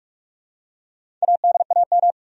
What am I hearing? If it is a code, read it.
Morse code, heard as a keyed tone, about 46 wpm, 700 Hz. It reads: ADAM
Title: USA Common Names - 45wpm